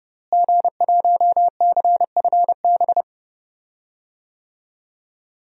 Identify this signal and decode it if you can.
Morse code, heard as a keyed tone, about 30 wpm, 700 Hz. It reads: G1CFB
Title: DXCC-100 Call Signs - 5 Characters - Repeated 2x - 30wpm